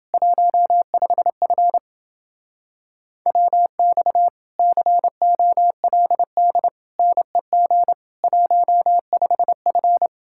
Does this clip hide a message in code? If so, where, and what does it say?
Yes: Morse code from 0.1 to 10.1 s, heard as a keyed tone, about 29 wpm, 700 Hz. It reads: 15F WX COLD NEG 15F